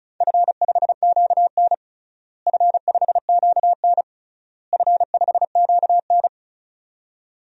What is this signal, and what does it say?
Morse code, heard as a keyed tone, about 36 wpm, 690 Hz. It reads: F5QN F5QN F5QN
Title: DXCC-100 Call Signs - 4 Characters - Repeated 3x - 35wpm